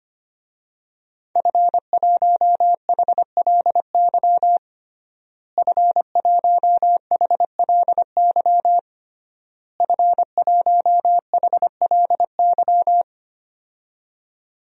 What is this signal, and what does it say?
Morse code, heard as a keyed tone, about 25 wpm, 700 Hz. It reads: F1HLY F1HLY F1HLY
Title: DXCC-100 Call Signs - 5 Characters - Repeated 3x - 25wpm